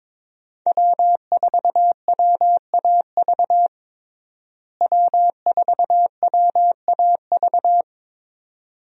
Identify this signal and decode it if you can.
Morse code, heard as a keyed tone, about 22 wpm, 700 Hz. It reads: W4WAV W4WAV
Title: USA Call Signs - 5 Characters - Repeated 2x - 22wpm